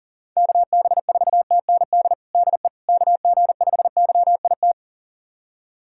Morse code, heard as a keyed tone, about 38 wpm, 690 Hz. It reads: KB4TND DE KG5YIT
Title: USA Callsign DE Callsign - 6 Characters - 40wpm